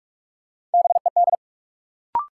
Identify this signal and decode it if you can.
Morse code, heard as a keyed tone, about 47 wpm, 700 Hz. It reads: BED
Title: Sentences from Top 500 Words - New Word Drill - 45wpm